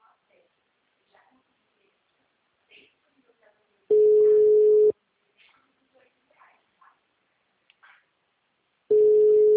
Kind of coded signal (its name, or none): telephone call progress tone